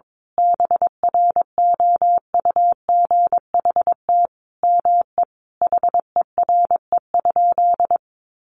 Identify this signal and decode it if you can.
Morse code, heard as a keyed tone, about 22 wpm, 700 Hz. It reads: BROUGHT ME HERE?
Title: Sentences from Top 1200 Words 22wpm